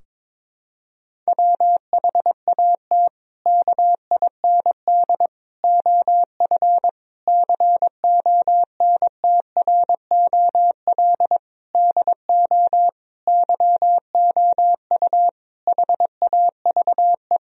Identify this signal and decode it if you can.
Morse code, heard as a keyed tone, about 22 wpm, 700 Hz. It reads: WHAT KIND OF CONTROL DO YOU HAVE
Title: Sentences from Top 1000 Words - Review Part 3 - 22wpm